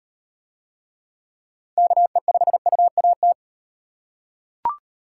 Morse code, heard as a keyed tone, about 36 wpm, 700 Hz. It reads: KE5UAT